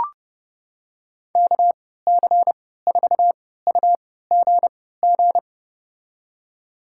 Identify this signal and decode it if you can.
Morse code, heard as a keyed tone, about 32 wpm, spaced as for 17 wpm, 700 Hz. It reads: KC4UGG
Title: USA Call Signs - 6 Characters - 3x Character Spacing - 30wpm